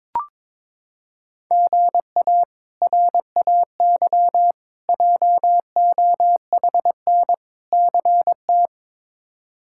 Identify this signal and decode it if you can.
Morse code, heard as a keyed tone, about 22 wpm, 700 Hz. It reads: GA RAY JOHN CT
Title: K1USN SST QSO Practice - 22wpm